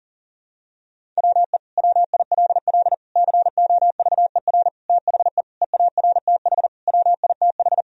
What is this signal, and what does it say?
Morse code, heard as a keyed tone, about 40 wpm, 700 Hz. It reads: WE WILL COVER THE EARTH WITH
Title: Sentences from Top 1000 Words - Review Part 1 - 40wpm